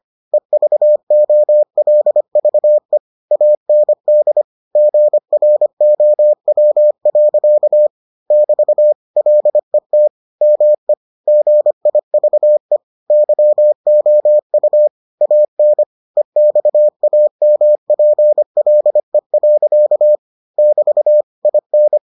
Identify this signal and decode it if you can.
Morse code, heard as a keyed tone, about 25 wpm, 600 Hz. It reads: EVOLVE AND GROW. <BT> LET ME GIVE YOU AN EXAMPLE. <BT> IN